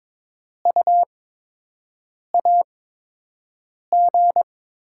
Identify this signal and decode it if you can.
Morse code, heard as a keyed tone, about 22 wpm, spaced as for 6 wpm, 700 Hz. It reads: U AG
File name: Top 20 QSO Elements - 8x Character Spacing - 22wpm